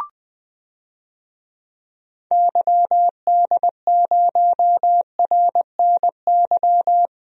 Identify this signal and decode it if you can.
Morse code, heard as a keyed tone, about 20 wpm, 700 Hz. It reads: YD0RNY
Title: DXCC-100 Call Signs - 6 Characters - Repeated 3x - 20wpm